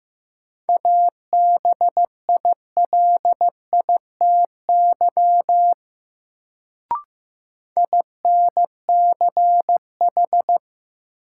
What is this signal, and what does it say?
Morse code, heard as a keyed tone, about 15 wpm, 700 Hz. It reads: ABILITY INCH
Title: Sentences from Top 1000 Words - Review Part 3 - New Word Drill - 15wpm